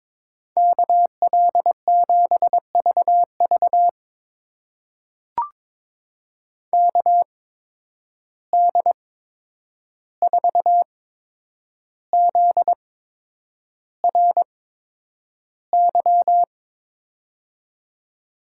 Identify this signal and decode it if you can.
Morse code, heard as a keyed tone, about 22 wpm, spaced as for 9 wpm, 700 Hz. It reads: KL7VV KD4ZRY
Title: USA Call Signs - 8x Character Spacing - 22wpm